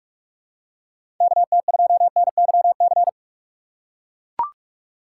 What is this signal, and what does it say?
Morse code, heard as a keyed tone, about 44 wpm, 700 Hz. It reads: KT2NYC
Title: USA Call Signs - 6 Characters - 6x Character Spacing - 45wpm